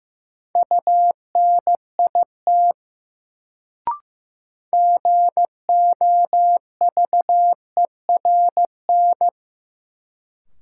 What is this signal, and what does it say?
Morse code, heard as a keyed tone, about 15 wpm, 700 Hz. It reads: UNIT GOVERN
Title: Sentences from Top 1000 Words - Review Part 2 - New Word Drill - 15wpm